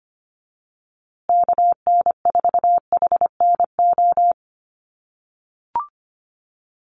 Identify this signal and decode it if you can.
Morse code, heard as a keyed tone, about 25 wpm, 700 Hz. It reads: KN4HNO
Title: USA Call Signs Repeated 3x 25wpm